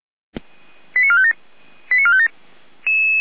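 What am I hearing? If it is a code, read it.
Five-tone selective call (ZVEI), heard as quick runs of five tones, tones of about 70 ms. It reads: 89457 89457
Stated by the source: ZVEI five-tone call received by SDR.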